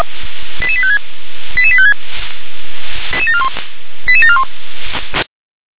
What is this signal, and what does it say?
Five-tone selective call (ZVEI), heard as quick runs of five tones, tones of about 70 ms. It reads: 88856 88856 88741